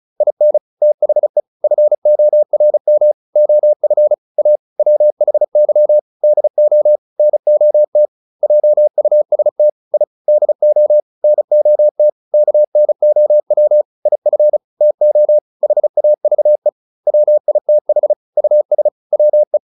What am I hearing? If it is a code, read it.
Morse code, heard as a keyed tone, about 34 wpm, 600 Hz. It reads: IN THE FORM OF A WHY DO NOT JUST I DO NOT KNOW IF TO HAVE WITH US WE
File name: WD403 35